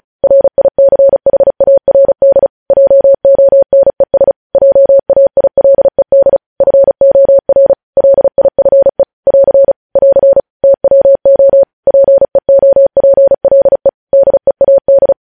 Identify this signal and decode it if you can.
Morse code, heard as a keyed tone, about 36 wpm, 550 Hz. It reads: RICHARD JONES JAILED FOR LIFE <AR> <AR> TWO PEOPLE DEAD
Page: News Headlines in Morse Code at 35 WPM